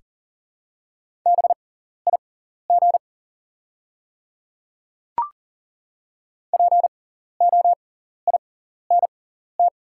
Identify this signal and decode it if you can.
Morse code, heard as a keyed tone, about 40 wpm, spaced as for 14 wpm, 700 Hz. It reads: BIG POINT
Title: 6x Character Spacing - Top 300 Words - 40wpm